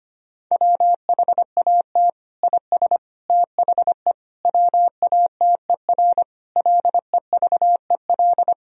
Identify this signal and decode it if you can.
Morse code, heard as a keyed tone, about 25 wpm, 700 Hz. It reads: WHAT IS THE WATER LEVEL
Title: Sentences from Top 1000 Words - Review Part 4 - 25wpm